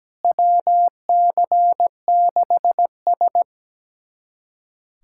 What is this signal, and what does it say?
Morse code, heard as a keyed tone, about 17 wpm, 700 Hz. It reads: WC6S
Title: USA Callsign DE Callsign - 4 Characters - 17wpm